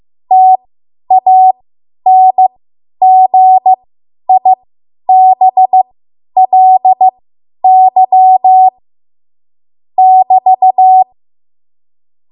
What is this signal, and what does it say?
Morse code, heard as a keyed tone, about 15 wpm, spaced as for 10 wpm, 750 Hz. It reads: TANGIBLY <BT>
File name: Morse / 10WPM